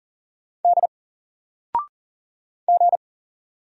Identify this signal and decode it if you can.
Morse code, heard as a keyed tone, about 39 wpm, 700 Hz. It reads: D G
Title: Single Letter-Number 40wpm